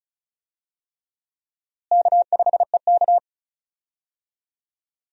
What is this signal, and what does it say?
Morse code, heard as a keyed tone, about 36 wpm, 700 Hz. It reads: K5EK
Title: USA Call Signs - 4 Characters - Repeated 3x - 35wpm